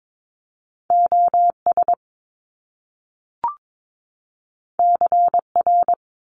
Morse code, heard as a keyed tone, about 22 wpm, 700 Hz. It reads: OS CR
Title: Top Two Letter Combinations in Words 22wpm